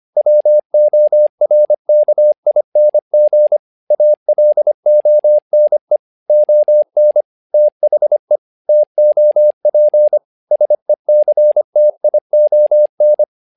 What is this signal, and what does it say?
Morse code, heard as a keyed tone, about 25 wpm, 600 Hz. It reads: WORKING ALONE ON THE TOP SECTION
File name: learning center / tower trouble / tower trouble 25wpm